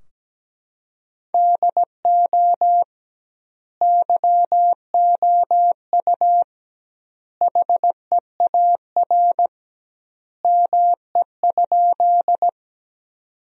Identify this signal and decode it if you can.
Morse code, heard as a keyed tone, about 17 wpm, 700 Hz. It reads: DO YOU HEAR ME?
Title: 2x Word-Spacing - Sentences from Top 300 Words - 17wpm